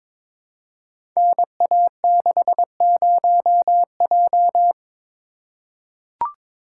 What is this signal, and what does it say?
Morse code, heard as a keyed tone, about 23 wpm, 700 Hz. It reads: NA60J
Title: Code Groups - 5 Letter:Numbers - 22wpm